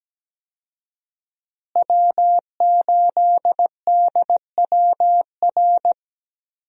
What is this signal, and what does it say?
Morse code, heard as a keyed tone, about 17 wpm, 700 Hz. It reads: W8DWR
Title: USA Call Signs - 5 Characters - Repeated 3x - 17wpm